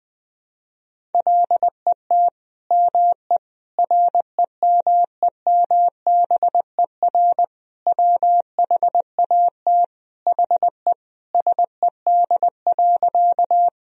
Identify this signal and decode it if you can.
Morse code, heard as a keyed tone, about 20 wpm, 700 Hz. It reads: LET ME REMEMBER WHAT HE SED.
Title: Sentences from Top 400 Words and Abbreviations 20wpm